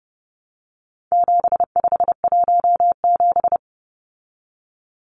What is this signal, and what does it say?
Morse code, heard as a keyed tone, about 30 wpm, 700 Hz. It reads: 7517
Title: Numbers - 4-Digit 30wpm